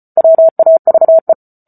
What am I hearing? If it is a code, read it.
Morse code, heard as a keyed tone, about 34 wpm, 650 Hz. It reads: WAVE